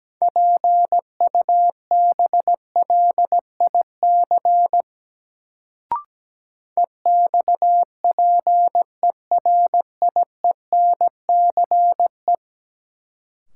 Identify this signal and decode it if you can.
Morse code, heard as a keyed tone, about 17 wpm, 700 Hz. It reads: PUBLIC EXPERIENCE
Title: Sentences from Top 600 Words - New Word Drill - 17wpm